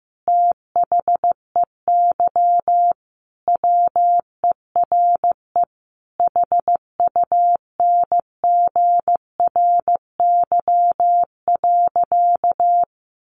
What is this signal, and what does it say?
Morse code, heard as a keyed tone, about 15 wpm, 700 Hz. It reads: THEY WERE HUNGRY.
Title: Sentences from Top 1400 Words 15wpm